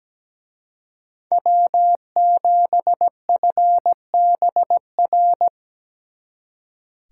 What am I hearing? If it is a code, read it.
Morse code, heard as a keyed tone, about 20 wpm, 700 Hz. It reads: W7FBR
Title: Speed Racing - USA 1x3 Call Signs 17wpm